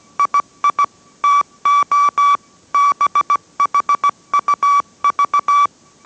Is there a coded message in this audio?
Yes: Morse code from 0.2 to 5.7 s, heard as a keyed tone, about 17 wpm, 1150 Hz. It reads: II TO BHUV